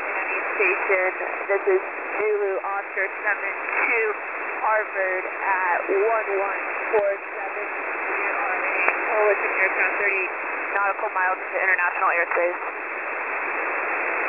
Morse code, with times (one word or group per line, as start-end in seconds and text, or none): none